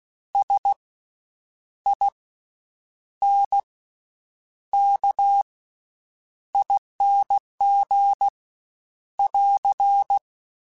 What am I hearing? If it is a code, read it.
Morse code, heard as a keyed tone, about 16 wpm, spaced as for 8 wpm, 800 Hz. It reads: SINKING <AR>
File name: Special INGs-mp3s / Special INGs-mp3s / Slow ing @15